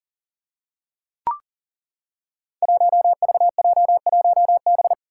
Morse code, heard as a keyed tone, about 38 wpm, 700 Hz. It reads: 1VJ1B
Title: Five Challenging Letter-Number Combinations 40wpm